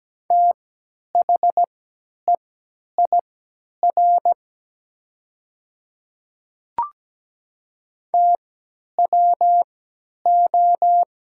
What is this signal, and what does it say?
Morse code, heard as a keyed tone, about 17 wpm, spaced as for 10 wpm, 700 Hz. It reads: THEIR TWO
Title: 3x Character Spacing - Top 100 Words - 17wpm